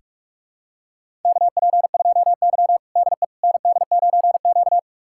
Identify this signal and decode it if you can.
Morse code, heard as a keyed tone, about 42 wpm, 700 Hz. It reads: KP2Y DE ND9Q
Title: USA Callsign DE Callsign - 4 Characters - 45wpm